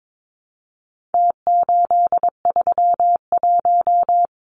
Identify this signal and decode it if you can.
Morse code, heard as a keyed tone, about 22 wpm, 700 Hz. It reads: T831
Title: Numbers - Contest Style - 4-Digit 22wpm